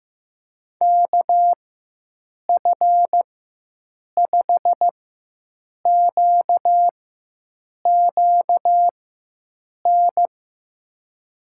Morse code, heard as a keyed tone, about 16 wpm, spaced as for 7 wpm, 700 Hz. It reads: KF5QQN
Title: USA Call Signs - 6 Characters - 4x Character Spacing - 15wpm